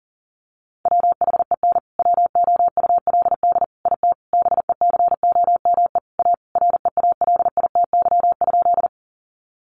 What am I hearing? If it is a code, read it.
Morse code, heard as a keyed tone, about 40 wpm, 700 Hz. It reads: WHEN WOULD IT BECOME A REALITY?
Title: Sentences from Top 1200 Words 40wpm